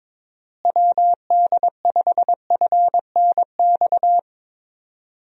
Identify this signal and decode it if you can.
Morse code, heard as a keyed tone, about 22 wpm, 700 Hz. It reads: WD5FNX
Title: USA Call Signs - 6 Characters - Repeated 2x - 22wpm